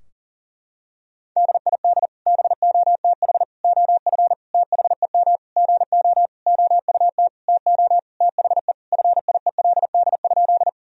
Morse code, heard as a keyed tone, about 40 wpm, 700 Hz. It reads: DID BOTH OF THEM GO OUT TO THE FIELD?
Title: Sentences from Top 1000 Words - Review Part 2 - 40wpm